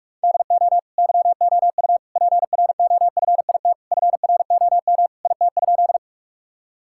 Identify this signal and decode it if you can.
Morse code, heard as a keyed tone, about 41 wpm, 700 Hz. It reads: DO YOU PROFIT FROM IT?
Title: Sentences from Top 1000 Words - Review Part 5 - 45wpm